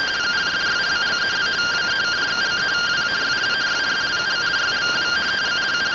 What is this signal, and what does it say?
Two-tone teleprinter signal, about 170 Hz shift.